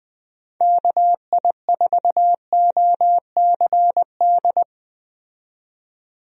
Morse code, heard as a keyed tone, about 20 wpm, 700 Hz. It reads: KI4OCD
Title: USA Call Signs - 6 Characters - 20wpm